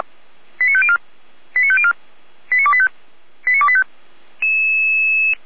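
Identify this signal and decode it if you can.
Five-tone selective call (ZVEI), heard as quick runs of five tones, tones of about 70 ms. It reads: 89504 89504 89285 89285
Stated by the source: ZVEI five-tone selective call received by SDR.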